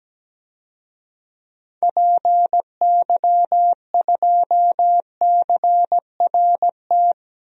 Morse code, heard as a keyed tone, about 17 wpm, 700 Hz. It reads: PY2CRT
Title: DXCC-100 Call Signs - 6 Characters - 17wpm